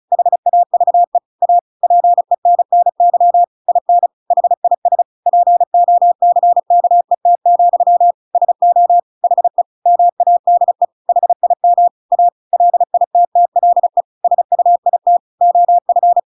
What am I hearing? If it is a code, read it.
Morse code, heard as a keyed tone, about 34 wpm, 700 Hz. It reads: HAVE A PENNY IN HIS POCKET, SO HE MADE HIM A LITTLE SUIT OF
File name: Pinocchio / Pinocchio ch8 35 35 ik1hge